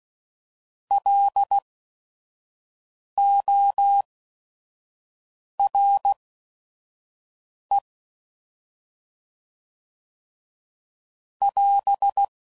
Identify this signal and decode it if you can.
Morse code, heard as a keyed tone, about 16 wpm, spaced as for 5 wpm, 800 Hz. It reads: LORE <AS>
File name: Prefixes / By First Letter / Ls